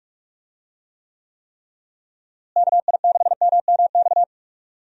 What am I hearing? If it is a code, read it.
Morse code, heard as a keyed tone, about 46 wpm, 700 Hz. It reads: KI6MMX
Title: USA Call Signs - 6 Characters - Repeated 3x - 45wpm